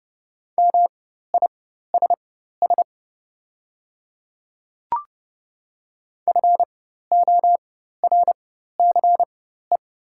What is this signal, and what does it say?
Morse code, heard as a keyed tone, about 29 wpm, spaced as for 14 wpm, 690 Hz. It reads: MISS FORCE
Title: 4x Character Spacing - Top 500 Words - 30wpm